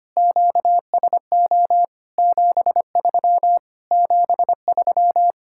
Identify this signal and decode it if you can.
Morse code, heard as a keyed tone, about 25 wpm, 700 Hz. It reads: QSO 73 73
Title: Speed-Racing Common QSO Phrases - 25wpm